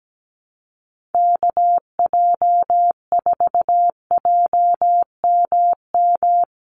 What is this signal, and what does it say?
Morse code, heard as a keyed tone, about 17 wpm, 700 Hz. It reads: KJ4JMM
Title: USA Call Signs Repeated 3x 17wpm